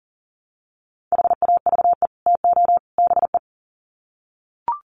Morse code, heard as a keyed tone, about 40 wpm, 700 Hz. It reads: HAVE TO BE
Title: Sets of 3 Words 40wpm